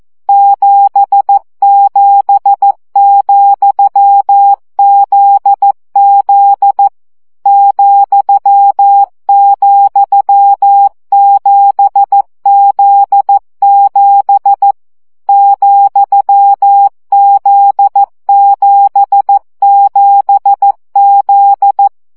Morse code, heard as a keyed tone, about 14 wpm, 800 Hz. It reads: 77,ZZ ,,7Z7 ,Z77Z
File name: CW gyakorlas z 7 vesszo 12 baud